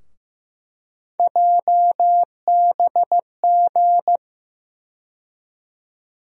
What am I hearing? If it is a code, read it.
Morse code, heard as a keyed tone, about 15 wpm, 700 Hz. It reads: JBG
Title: USA Call Sign Postfixes - 3 Letters 15wpm